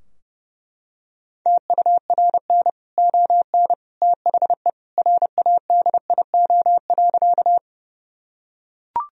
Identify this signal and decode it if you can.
Morse code, heard as a keyed tone, about 30 wpm, 700 Hz. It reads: TURN ON THE RADIO.
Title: Sentences from Top 1000 Words - Review Part 3 - 30wpm